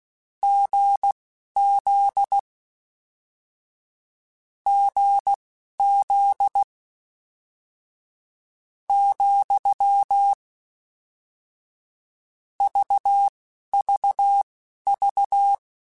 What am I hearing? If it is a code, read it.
Morse code, heard as a keyed tone, about 16 wpm, spaced as for 11 wpm, 800 Hz. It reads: GZ GZ , VVV
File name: Difficult Combos / Difficult Combos-mp3s / GZ GZ7 @10